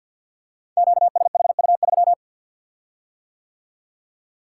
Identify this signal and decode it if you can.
Morse code, heard as a keyed tone, about 50 wpm, 700 Hz. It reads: QSHU3
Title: Code Groups - 5 Letter:Numbers - 50wpm